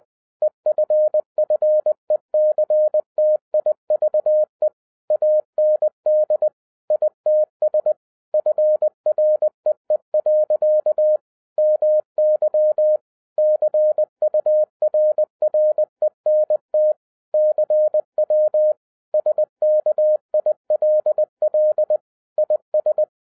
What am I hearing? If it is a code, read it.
Morse code, heard as a keyed tone, about 20 wpm, 610 Hz. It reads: EFFECTIVE AND ITS FREE. MY CURRENT CW SKILL IS
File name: 2012-01-mp3 / n05-CAME BACK TO HAM RADIO AFTER 30 YEARS OF INTERVAL-20wpm